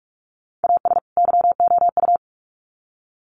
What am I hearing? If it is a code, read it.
Morse code, heard as a keyed tone, about 44 wpm, 700 Hz. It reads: AS YOU
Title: Sets of 2 Words 45wpm